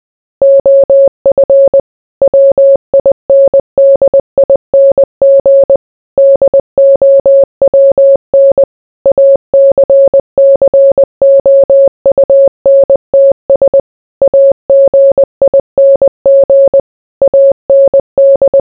Morse code, heard as a keyed tone, about 20 wpm, 550 Hz. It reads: OF WINDING DOWN ACCOUNTS AGING AND